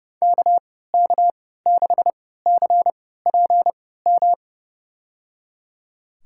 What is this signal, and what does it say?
Morse code, heard as a keyed tone, about 30 wpm, spaced as for 17 wpm, 700 Hz. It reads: KK6CPM
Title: USA Call Signs - 3x Character Spacing - 30wpm